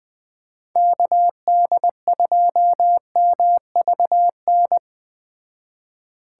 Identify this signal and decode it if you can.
Morse code, heard as a keyed tone, about 19 wpm, 700 Hz. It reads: KD2MVN